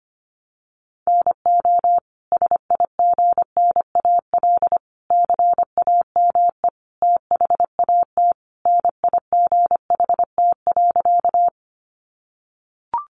Morse code, heard as a keyed tone, about 25 wpm, 700 Hz. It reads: NO SIGNAL CAME THAT NIGHT.
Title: Sentences from Top 1400 Words 25wpm